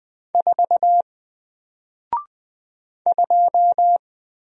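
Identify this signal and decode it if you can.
Morse code, heard as a keyed tone, about 20 wpm, 700 Hz. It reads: 4 2
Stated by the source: Single letter and number Morse drill at 20 wpm, from N4LCD.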